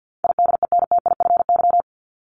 Morse code, heard as a keyed tone, about 50 wpm, 700 Hz. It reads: IDENTIFY
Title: Top 751-1000 Words 50wpm